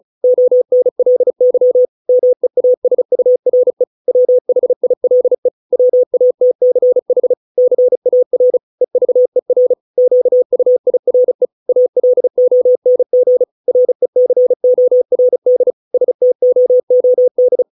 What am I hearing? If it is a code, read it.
Morse code, heard as a keyed tone, about 35 wpm, 500 Hz. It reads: ONLY MEASURE WHILE WATCH CAR EVER QUIRE ALONG RECORD STOOD